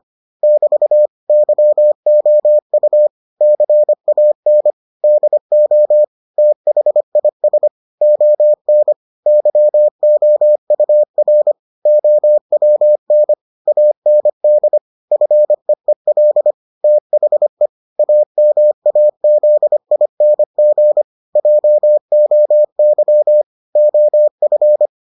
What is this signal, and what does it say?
Morse code, heard as a keyed tone, about 25 wpm, 600 Hz. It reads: <BT> YOU CAN DO THIS ON YOUR OWN AND FEEL THE AMAZING JOY OF